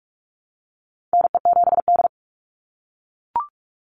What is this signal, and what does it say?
Morse code, heard as a keyed tone, about 44 wpm, 700 Hz. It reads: NE7D
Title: Call Signs - 45wpm